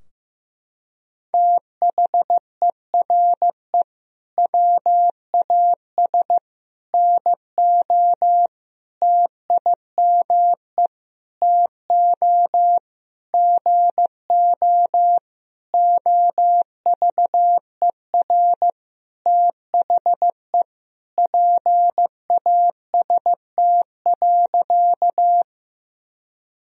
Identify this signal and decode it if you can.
Morse code, heard as a keyed tone, about 15 wpm, 700 Hz. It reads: THERE WAS NO TIME TO GO OVER THE PAST.
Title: Sentences from Top 1000 Words - Review Part 3 - 15wpm